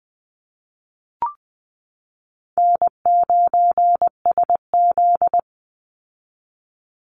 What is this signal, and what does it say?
Morse code, heard as a keyed tone, about 19 wpm, 700 Hz. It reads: N9SZ
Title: Easy Call Signs Repeated 2x - 20wpm